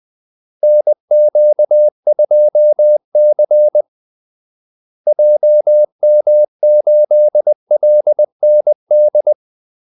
Morse code, heard as a keyed tone, about 20 wpm, 600 Hz. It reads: NQ2C JM8LND
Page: Kaldesignaler 20 wpm